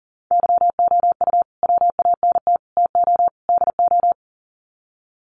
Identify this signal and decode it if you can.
Morse code, heard as a keyed tone, about 40 wpm, 700 Hz. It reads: YOU WANT TO DO
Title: Sets of 4 Words 40wpm